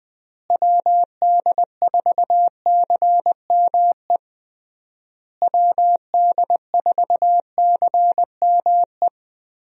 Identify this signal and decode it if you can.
Morse code, heard as a keyed tone, about 21 wpm, 700 Hz. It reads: WD4CME WD4CME
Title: USA Call Signs - 6 Characters - Repeated 3x - 20wpm